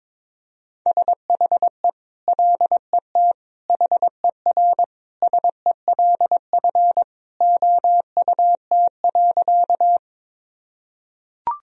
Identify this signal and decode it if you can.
Morse code, heard as a keyed tone, about 22 wpm, 700 Hz. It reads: SHE LET HERSELF OUT.
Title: Sentences from Top 1000 Words - Review Part 1 - 22wpm